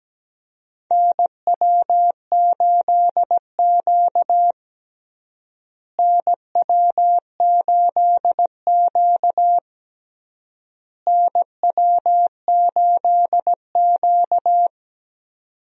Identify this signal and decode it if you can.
Morse code, heard as a keyed tone, about 17 wpm, 700 Hz. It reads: NW8Q NW8Q NW8Q
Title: USA Call Signs - 4 Characters - Repeated 3x - 17wpm